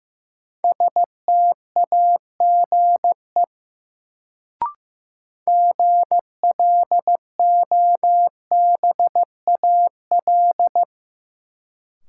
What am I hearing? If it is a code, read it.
Morse code, heard as a keyed tone, about 15 wpm, 700 Hz. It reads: STAGE GLOBAL
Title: Sentences from Top 1200 Words - New Word Drill Alt - 15wpm